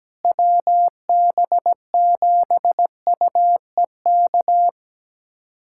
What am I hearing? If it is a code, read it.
Morse code, heard as a keyed tone, about 17 wpm, 700 Hz. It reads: WB7UEK
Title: USA Callsign DE Callsign 17wpm